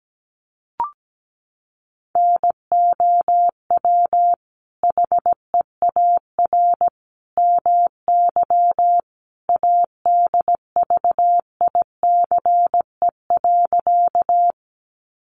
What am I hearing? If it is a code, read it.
Morse code, heard as a keyed tone, about 17 wpm, 700 Hz. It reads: NOW HEAR MY ADVICE.
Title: Sentences from Top 1100 Words 17wpm